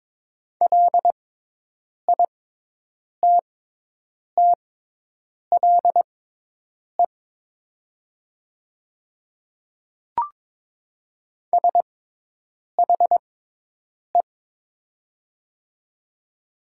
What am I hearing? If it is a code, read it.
Morse code, heard as a keyed tone, about 22 wpm, spaced as for 8 wpm, 700 Hz. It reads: LITTLE SHE
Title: 6x Character Spacing - Top 200 Words - 22wpm